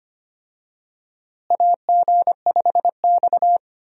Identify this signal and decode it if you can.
Morse code, heard as a keyed tone, about 25 wpm, 700 Hz. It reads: AG5X
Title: USA Call Signs - 4 Characters - Repeated 3x - 25wpm